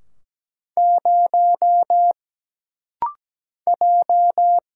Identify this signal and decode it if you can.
Morse code, heard as a keyed tone, about 17 wpm, 700 Hz. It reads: 0 J
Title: Single Letter-Number - Rapid-Fire 17wpm